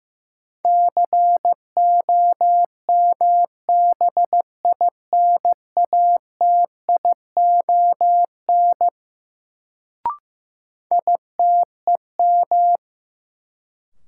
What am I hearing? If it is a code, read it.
Morse code, heard as a keyed tone, about 15 wpm, 700 Hz. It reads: COMBINATION ITEM
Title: Sentences from Top 1000 Words - Review Part 6 - New Word Drill - 15wpm